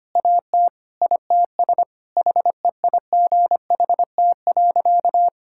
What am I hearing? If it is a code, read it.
Morse code, heard as a keyed tone, about 25 wpm, 700 Hz. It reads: AT ITS HEIGHT.
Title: Sentences from Top 1600 Words 25wpm